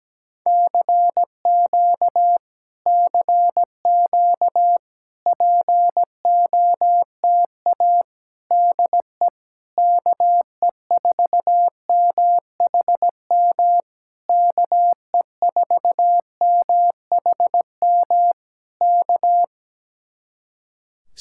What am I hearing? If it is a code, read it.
Morse code, heard as a keyed tone, about 17 wpm, 700 Hz. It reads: CQ CQ POTA DE KE4MHM KE4MHM K
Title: Speed-Racing Common QSO Phrases Encore - 17wpm